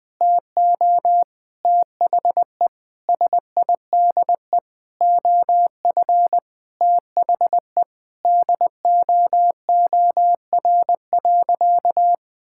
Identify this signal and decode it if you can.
Morse code, heard as a keyed tone, about 21 wpm, 700 Hz. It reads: TO THE SIDE OF THE DOOR.
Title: Sentences with 5-Word Phrases - 20wpm